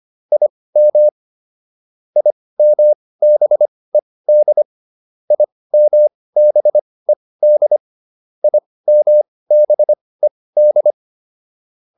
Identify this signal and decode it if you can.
Morse code, heard as a keyed tone, about 25 wpm, spaced as for 18 wpm, 600 Hz. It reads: IM IMBED IMBED IMBED